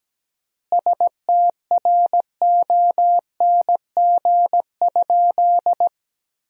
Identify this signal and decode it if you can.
Morse code, heard as a keyed tone, about 17 wpm, 700 Hz. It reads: STRONG?
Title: 4x Word-Spacing - Sentences from Top 500 Words - 17wpm